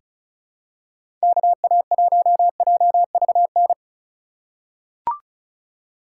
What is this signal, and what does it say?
Morse code, heard as a keyed tone, about 36 wpm, 700 Hz. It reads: KA1JVN
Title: USA Call Signs - 6 Characters - 2x Character Spacing - 35wpm